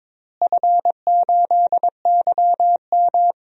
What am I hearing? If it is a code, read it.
Morse code, heard as a keyed tone, about 22 wpm, 700 Hz. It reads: F8YM
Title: DXCC-100 Call Signs - 4 Characters - Repeated 3x - 22wpm